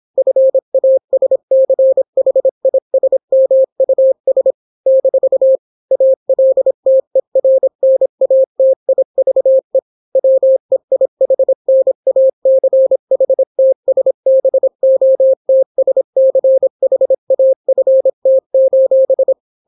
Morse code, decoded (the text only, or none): FASCHISMUS - ALTERNATIVE WEIHNACHTSBOTSCHAFT: